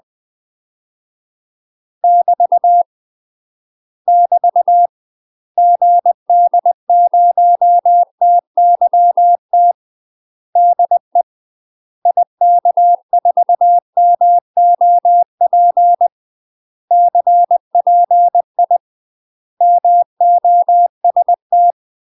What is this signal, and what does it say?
Morse code, heard as a keyed tone, about 20 wpm, 700 Hz. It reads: <BT> <BT> GD0TYT DE IK4MOP CPI MOST
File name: UK MASTER QSOs TRAINING / 20WPM / UK MASTER QSOs 0011